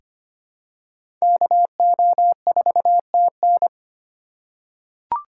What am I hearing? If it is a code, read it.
Morse code, heard as a keyed tone, about 25 wpm, 700 Hz. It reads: KO4TN
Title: USA Call Signs - 5 Characters - 4x Character Spacing - 25wpm